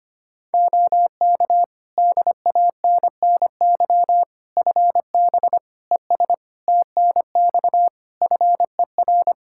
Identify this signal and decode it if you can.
Morse code, heard as a keyed tone, about 25 wpm, 700 Hz. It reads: OK DANNY FB ES TNX FER
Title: Practice QSO - Exchange 2 - Info - 25wpm